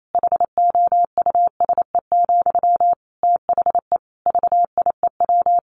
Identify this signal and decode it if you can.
Morse code, heard as a keyed tone, about 28 wpm, 700 Hz. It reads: HOUSE, THE VIEW
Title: Sentences from Top 1100 Words 28wpm